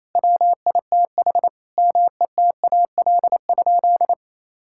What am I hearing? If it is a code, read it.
Morse code, heard as a keyed tone, about 28 wpm, 700 Hz. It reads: WITH METAL?
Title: Sentences from Top 1600 Words 28wpm